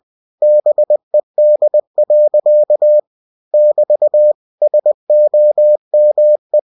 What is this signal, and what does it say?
Morse code, heard as a keyed tone, about 20 wpm, 600 Hz. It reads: BED. <BT> SOME
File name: cw-books / winnie-the-pooh / winnie-20wpm-page-0038